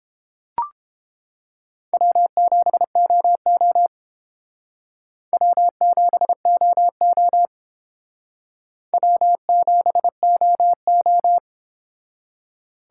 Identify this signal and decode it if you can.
Morse code, heard as a keyed tone, about 29 wpm, 700 Hz. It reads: W7OO W7OO W7OO